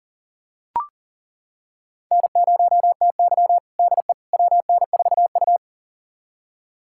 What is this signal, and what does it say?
Morse code, heard as a keyed tone, about 40 wpm, 700 Hz. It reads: N0TY DE WN4U
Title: USA Callsign DE Callsign - 4 Characters - 40wpm